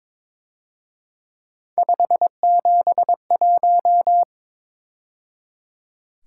Morse code, heard as a keyed tone, about 26 wpm, 700 Hz. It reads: 571
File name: Speed-Racing - Three Letter-Number Combinations - 22wpm